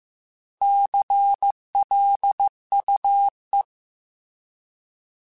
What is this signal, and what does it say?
Morse code, heard as a keyed tone, about 15 wpm, 800 Hz. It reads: CLUE